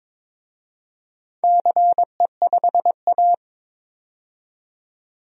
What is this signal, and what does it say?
Morse code, heard as a keyed tone, about 22 wpm, 700 Hz. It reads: CE5A
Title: DXCC-100 Call Signs - 4 Characters - 22wpm